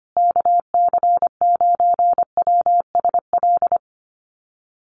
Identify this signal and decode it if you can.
Morse code, heard as a keyed tone, about 25 wpm, 700 Hz. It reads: KC9WSL
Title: USA Call Signs Repeated 2x 25wpm